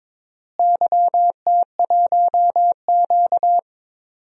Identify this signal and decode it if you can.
Morse code, heard as a keyed tone, about 22 wpm, 700 Hz. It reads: YT1Q
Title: DXCC-100 Call Signs - 4 Characters - Repeated 3x - 22wpm